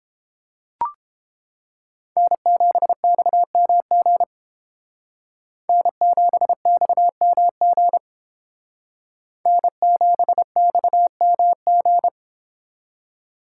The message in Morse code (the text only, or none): N7XMG N7XMG N7XMG